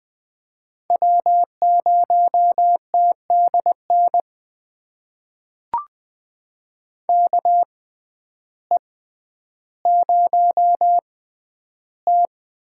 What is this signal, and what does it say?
Morse code, heard as a keyed tone, about 20 wpm, spaced as for 11 wpm, 700 Hz. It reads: W0TDN KE0T
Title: USA Call Signs - 5 Characters - 6x Character Spacing - 20wpm